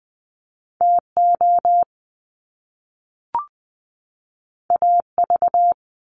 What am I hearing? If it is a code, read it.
Morse code, heard as a keyed tone, about 20 wpm, 700 Hz. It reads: TO AV
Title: Top Two Letter Combinations in Words 20wpm